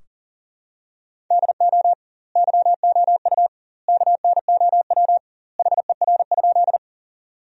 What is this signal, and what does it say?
Morse code, heard as a keyed tone, about 40 wpm, 700 Hz. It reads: DO YOU KNOW HER?
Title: 2x Word-Spacing - Sentences from Top 100 Words - 40wpm